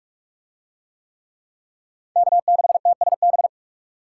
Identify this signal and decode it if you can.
Morse code, heard as a keyed tone, about 38 wpm, 700 Hz. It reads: K6TSB